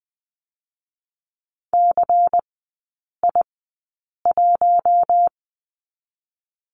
Morse code, heard as a keyed tone, about 20 wpm, 700 Hz. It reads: C I 1
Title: 114 Lesson 19e - Letter w - Three Characters 20wpm